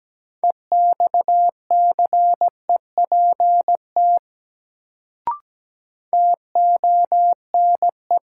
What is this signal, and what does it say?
Morse code, heard as a keyed tone, about 17 wpm, 700 Hz. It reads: EXCEPT TONE
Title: Sentences from Top 1600 Words - New Word Drill - 17wpm